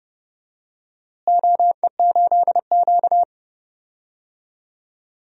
Morse code, heard as a keyed tone, about 30 wpm, 700 Hz. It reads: OE8Q
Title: DXCC-100 Call Signs - 4 Characters - Repeated 2x - 30wpm